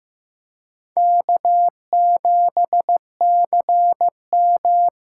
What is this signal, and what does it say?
Morse code, heard as a keyed tone, about 15 wpm, 700 Hz. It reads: K7CM
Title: USA Call Signs - 4 Characters - 4x Character Spacing - 15wpm